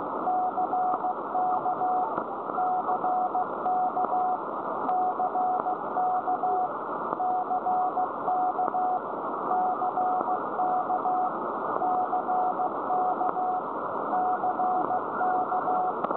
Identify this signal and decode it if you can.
Morse code, heard as a keyed tone, about 15 wpm, 740 Hz, with noise. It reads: CK CK CK CK CK CK CK
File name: storage / websdr G8HXT Wheathampstead 1347km 14149.0kHz